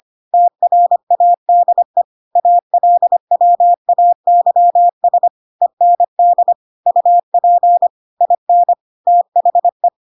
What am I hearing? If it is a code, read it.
Morse code, heard as a keyed tone, about 25 wpm, 700 Hz. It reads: TRADE ALWAYS END UP IN THE
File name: Pinocchio / Pinocchio ch4 25 25 ik1hge